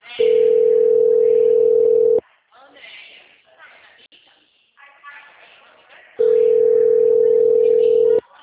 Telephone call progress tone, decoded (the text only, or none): ringback tone